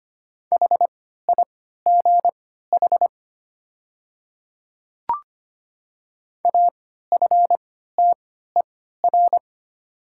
Morse code, heard as a keyed tone, about 25 wpm, spaced as for 14 wpm, 690 Hz. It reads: HIGH AFTER
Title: 3x Character Spacing - Top 200 Words - 25wpm